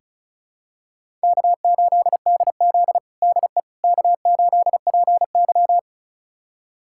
Morse code, heard as a keyed tone, about 34 wpm, 700 Hz. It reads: K8DZ DE K8PY
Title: USA Callsign DE Callsign - 4 Characters - 35wpm